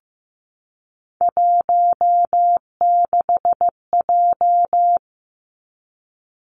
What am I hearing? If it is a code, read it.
Morse code, heard as a keyed tone, about 15 wpm, 700 Hz. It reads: 16J
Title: Three Challenging Letter-Number Combinations 15wpm